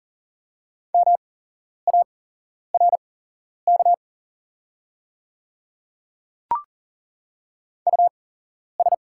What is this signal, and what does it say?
Morse code, heard as a keyed tone, about 38 wpm, spaced as for 11 wpm, 690 Hz. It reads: MARK US
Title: 8x Character Spacing - Top 300 Words - 40wpm